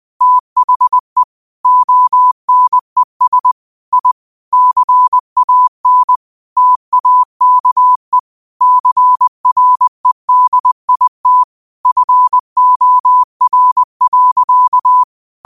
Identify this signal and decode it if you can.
Morse code, heard as a keyed tone, about 20 wpm, 1000 Hz. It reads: THE ONES I CAN TAKE CREDIT FOR.
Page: 20 WPM morse code quotes for Tue, 12 Nov 2024.